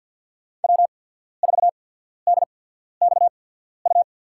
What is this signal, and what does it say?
Morse code, heard as a keyed tone, about 53 wpm, spaced as for 14 wpm, 700 Hz. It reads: W4DXU
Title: USA Call Signs - 5 Characters - 8x Character Spacing - 50wpm